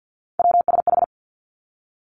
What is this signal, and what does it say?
Morse code, heard as a keyed tone, about 50 wpm, 700 Hz. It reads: WSH
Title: Three Challenging Letter-Number Combinations 50wpm